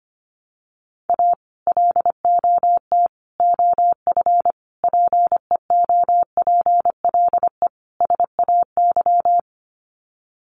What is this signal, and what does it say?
Morse code, heard as a keyed tone, about 25 wpm, 700 Hz. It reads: A LOT OF PEOPLE SAY
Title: Sets of 5 Words Encore2 25wpm